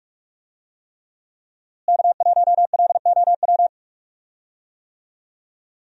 Morse code, heard as a keyed tone, about 46 wpm, 700 Hz. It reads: K1LOW
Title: USA Call Signs - 5 Characters - Repeated 3x - 45wpm